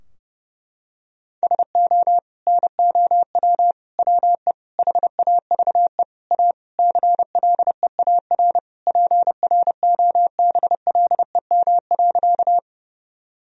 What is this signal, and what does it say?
Morse code, heard as a keyed tone, about 30 wpm, 700 Hz. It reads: SO NOW WE HAVE A CLEAR PROBLEM.
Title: Sentences from Top 1000 Words - Review Part 2 - 30wpm